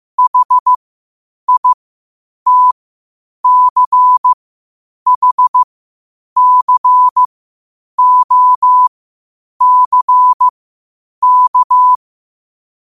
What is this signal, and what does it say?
Morse code, heard as a keyed tone, about 15 wpm, spaced as for 8 wpm, 1000 Hz. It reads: HITCHCOCK
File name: qotdfiles / 8wpm